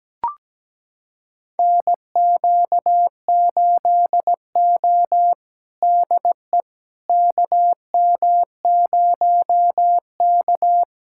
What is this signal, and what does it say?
Morse code, heard as a keyed tone, about 17 wpm, 700 Hz. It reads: NQ8O DE KM0K
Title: USA Callsign DE Callsign - 4 Characters - 17wpm